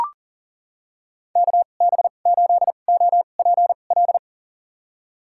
Morse code, heard as a keyed tone, about 40 wpm, spaced as for 29 wpm, 700 Hz. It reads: KB8OPL
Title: USA Call Signs - 6 Characters - 2x Character Spacing - 40wpm